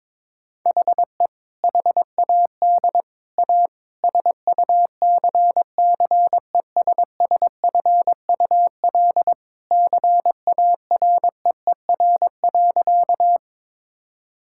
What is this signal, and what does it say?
Morse code, heard as a keyed tone, about 22 wpm, 700 Hz. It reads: HE HAD A SUCCESSFUL CAREER.
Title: Sentences from Top 1000 Words - Review Part 6 - 22wpm